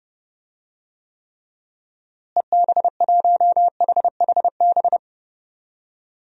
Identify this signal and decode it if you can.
Morse code, heard as a keyed tone, about 32 wpm, 700 Hz. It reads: EB1HHB
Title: DXCC-100 Call Signs - 6 Characters - Repeated 2x - 30wpm